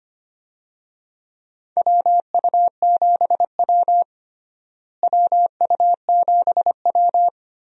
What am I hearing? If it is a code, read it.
Morse code, heard as a keyed tone, about 25 wpm, 700 Hz. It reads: WU7W WU7W
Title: USA Call Signs - 4 Characters - Repeated 3x - 25wpm